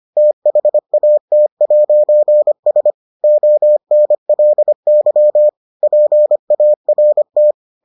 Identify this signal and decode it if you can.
Morse code, heard as a keyed tone, about 25 wpm, 600 Hz. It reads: THAT'S ONLY PART